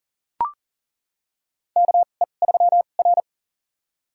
Morse code, heard as a keyed tone, about 38 wpm, spaced as for 28 wpm, 700 Hz. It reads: KE3R
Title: USA Call Signs - 2x Character Spacing - 40wpm